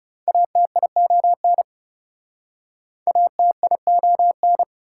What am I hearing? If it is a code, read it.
Morse code, heard as a keyed tone, about 39 wpm, 690 Hz. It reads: ATION ATION